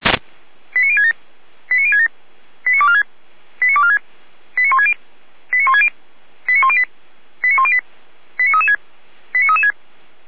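Five-tone selective call (ZVEI), heard as quick runs of five tones, tones of about 70 ms. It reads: 89067 89067 89246 89246 89160 89160 89108 89108 89336 89336
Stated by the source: ZVEI five-tone selective call 89099 received on an SDR, from feuerwehr-orlamuende.org (Feuerwehr Orlamünde fire brigade).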